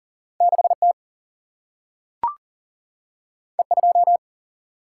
Morse code, heard as a keyed tone, about 40 wpm, 700 Hz. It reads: 6T E2
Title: Single-Letter Single-Number 40wpm